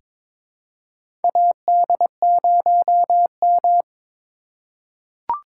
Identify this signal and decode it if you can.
Morse code, heard as a keyed tone, about 22 wpm, 700 Hz. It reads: AD0M
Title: USA Call Signs - 4 Characters - 3x Character Spacing - 22wpm